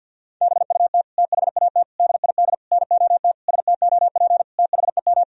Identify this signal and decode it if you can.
Morse code, heard as a keyed tone, about 48 wpm, 700 Hz. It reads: BUT THAT DID NOT STOP THEM
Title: Sentences from Top 1000 Words - Review Part 1 - 50wpm